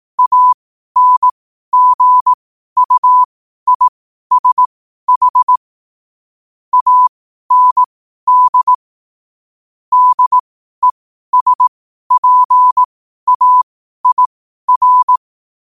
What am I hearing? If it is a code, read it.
Morse code, heard as a keyed tone, about 18 wpm, spaced as for 12 wpm, 1000 Hz. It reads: ANGUISH AND DESPAIR
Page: New quotes every day in morse code at 12 Words per minute.